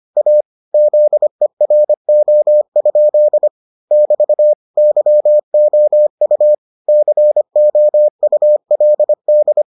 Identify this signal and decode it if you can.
Morse code, heard as a keyed tone, about 25 wpm, 600 Hz. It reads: A ZERO? <BT> YOU COULD